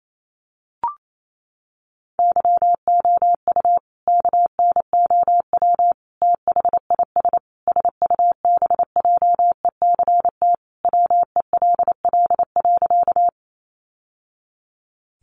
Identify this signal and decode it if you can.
Morse code, heard as a keyed tone, about 28 wpm, 700 Hz. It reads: YOU KNOW THIS SUBJECT WELL.
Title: Sentences from Top 800 Words 28wpm